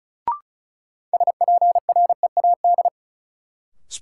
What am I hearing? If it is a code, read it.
Morse code, heard as a keyed tone, about 34 wpm, 700 Hz. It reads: SPREAD
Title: Sentences from Top 1200 Words - New Word Drill - 35wpm